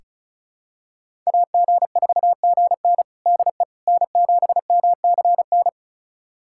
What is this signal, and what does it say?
Morse code, heard as a keyed tone, about 34 wpm, 700 Hz. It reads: AG4GN DE N7MCN
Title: USA Callsign DE Callsign - 5 Characters - 35wpm